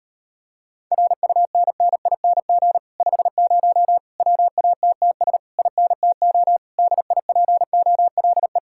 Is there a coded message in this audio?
Yes: Morse code from 0.9 to 8.6 s, heard as a keyed tone, about 33 wpm, 700 Hz. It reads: RUNNING 50 WATTS INTO DIPOLE